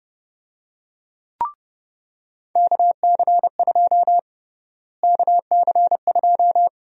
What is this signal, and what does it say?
Morse code, heard as a keyed tone, about 30 wpm, 700 Hz. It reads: KC2 KC2
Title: USA Call Sign Prefixes - Repeated 2x - 30wpm